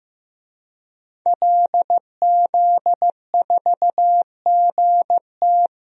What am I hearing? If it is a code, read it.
Morse code, heard as a keyed tone, about 15 wpm, 700 Hz. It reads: LZ4GT
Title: DXCC-100 Call Signs - 5 Characters - 15wpm